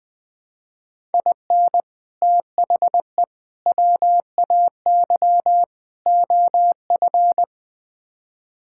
Morse code, heard as a keyed tone, about 21 wpm, 700 Hz. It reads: IN THE WAY OF
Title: Build A Phrase - 4-Word Phrases - 20wpm